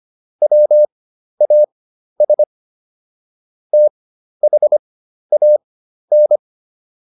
Morse code, heard as a keyed tone, about 25 wpm, spaced as for 12 wpm, 600 Hz. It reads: WAS THAN